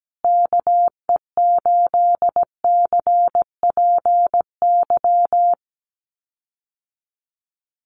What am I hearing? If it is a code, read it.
Morse code, heard as a keyed tone, about 17 wpm, 700 Hz. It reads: KE8CPY
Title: USA Call Signs Repeated 3x 17wpm